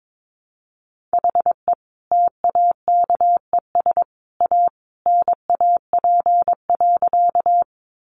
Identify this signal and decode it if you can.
Morse code, heard as a keyed tone, about 22 wpm, 700 Hz. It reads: HE TAKES A NAP.
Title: Easy Phrases 22wpm